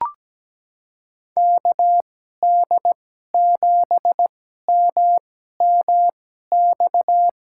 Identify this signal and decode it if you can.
Morse code, heard as a keyed tone, about 17 wpm, spaced as for 12 wpm, 700 Hz. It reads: KD7MMX
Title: USA Call Signs - 6 Characters - 2x Character Spacing - 17wpm